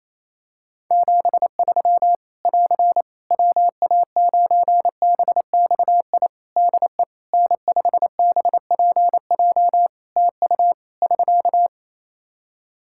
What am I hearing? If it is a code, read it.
Morse code, heard as a keyed tone, about 28 wpm, 700 Hz. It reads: 73 <AR> WA9BXI DE N5BPJ TU <SK>
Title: K1USN SST QSO Practice 28wpm